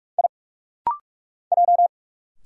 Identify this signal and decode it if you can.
Morse code, heard as a keyed tone, about 46 wpm, 690 Hz. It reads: I J
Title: Single Letters - Rapid-Fire 45wpm